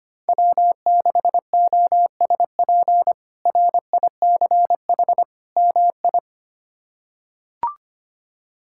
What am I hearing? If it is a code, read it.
Morse code, heard as a keyed tone, about 25 wpm, 700 Hz. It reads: W6OSP RICH MI
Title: K1USN SST QSO Practice - 25wpm